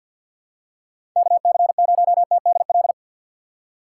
Morse code, heard as a keyed tone, about 46 wpm, 700 Hz. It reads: KC0TDL
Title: USA Call Signs - 6 Characters - 3x Character Spacing - 50wpm